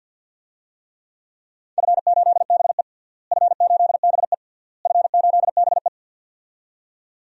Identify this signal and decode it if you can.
Morse code, heard as a keyed tone, about 50 wpm, 700 Hz. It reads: F8BE F8BE F8BE
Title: DXCC-100 Call Signs - 4 Characters - Repeated 3x - 50wpm